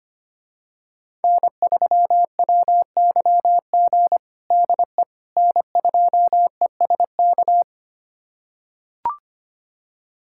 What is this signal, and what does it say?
Morse code, heard as a keyed tone, about 25 wpm, 700 Hz. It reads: N3WYG DE N2ESK
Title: USA Callsign DE Callsign - 5 Characters - 25wpm